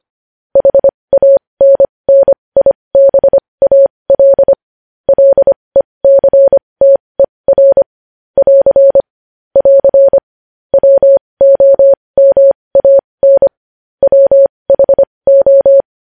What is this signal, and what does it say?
Morse code, heard as a keyed tone, about 25 wpm, spaced as for 20 wpm, 550 Hz. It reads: HANNIBAL LECTER <AR> <AR> WOMAN WHO
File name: podcast / News headlines-1x-25x20